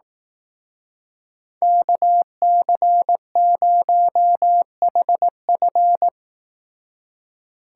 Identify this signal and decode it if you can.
Morse code, heard as a keyed tone, about 18 wpm, 700 Hz. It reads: KC0HF
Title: Speed Racing - USA 2x2 Call Signs 15wpm